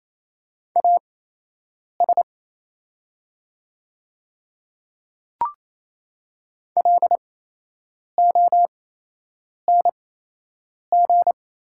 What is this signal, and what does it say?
Morse code, heard as a keyed tone, about 28 wpm, spaced as for 8 wpm, 700 Hz. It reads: AS LONG
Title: 8x Character Spacing - Top 100 Words - 28wpm